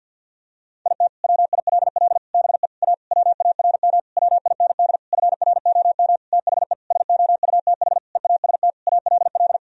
Morse code, heard as a keyed tone, about 52 wpm, 700 Hz. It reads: IT WILL BE A WARM WIND FROM THE SOUTH EAST ALL
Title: Sentences from Top 1000 Words - Review Part 3 - 50wpm